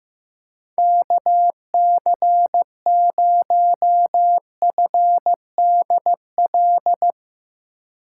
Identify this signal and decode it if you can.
Morse code, heard as a keyed tone, about 15 wpm, 700 Hz. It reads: KC0FDL
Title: USA Call Signs - 6 Characters - 4x Character Spacing - 15wpm